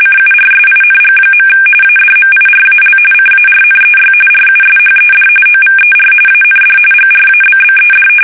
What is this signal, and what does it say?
RTTY, about 45.45 baud. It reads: CURSING. ADMIRAL CARNEY RUSHED OVER AND GRABBED
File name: fsk850 / battleshipnj